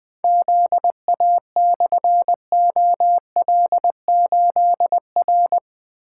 Morse code, heard as a keyed tone, about 20 wpm, 700 Hz. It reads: ZA/OL8R
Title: Difficult Call Signs 20wpm